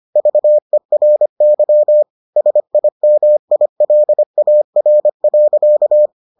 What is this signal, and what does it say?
Morse code, heard as a keyed tone, about 25 wpm, 600 Hz. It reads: VERY SIMILAR.